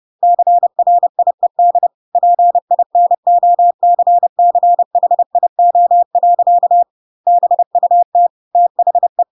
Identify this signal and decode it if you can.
Morse code, heard as a keyed tone, about 30 wpm, 700 Hz. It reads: CRIED PINOCCHIO. BUT THE
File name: Pinocchio / Pinocchio ch4 30 30 ik1hge